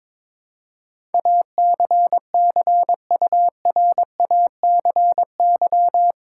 Morse code, heard as a keyed tone, about 22 wpm, 700 Hz. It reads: ACCURACY
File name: practice / 22wpm / Eight Letter Words - 22wpm